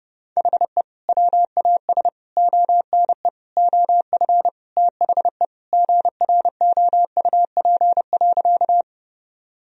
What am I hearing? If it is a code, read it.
Morse code, heard as a keyed tone, about 32 wpm, 700 Hz. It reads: HE WAS ONE OF THE GROUP.
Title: Sentences from Top 1000 Words - Review Part 1 - 30wpm